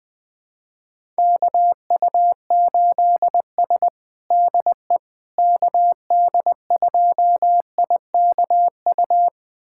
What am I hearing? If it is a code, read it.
Morse code, heard as a keyed tone, about 20 wpm, 700 Hz. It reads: KU8S DE KD2IKU
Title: USA Callsign DE Callsign 20wpm